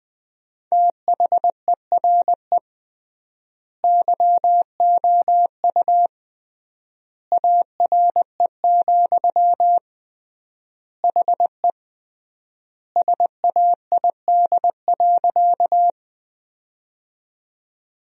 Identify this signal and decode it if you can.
Morse code, heard as a keyed tone, about 20 wpm, 700 Hz. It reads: THERE YOU ARE, HE SAID.
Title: 3x Word-Spacing - Sentences from Top 100 Words - 20wpm